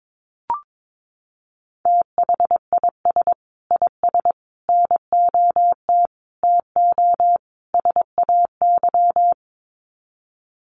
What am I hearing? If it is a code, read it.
Morse code, heard as a keyed tone, about 22 wpm, 700 Hz. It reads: THIS IS NOT TO SAY
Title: Sets of 5 Words Encore 22wpm